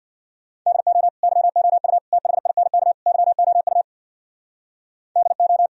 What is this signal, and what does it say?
Morse code, heard as a keyed tone, about 66 wpm, 690 Hz. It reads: DO YOU THINK YOU DO